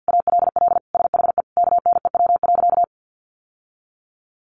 Morse code, heard as a keyed tone, about 50 wpm, 700 Hz. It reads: ALL SHE KNEW.
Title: Sentences from Top 1300 Words 50wpm